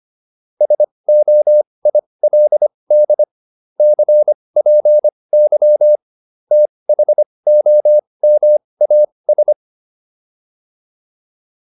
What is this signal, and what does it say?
Morse code, heard as a keyed tone, about 25 wpm, spaced as for 20 wpm, 600 Hz. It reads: SOILD CPY THOMAS